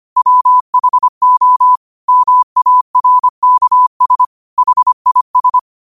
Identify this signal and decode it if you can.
Morse code, heard as a keyed tone, about 25 wpm, 1000 Hz. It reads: WHO MARKS HIS
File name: qotdfiles / 25wpm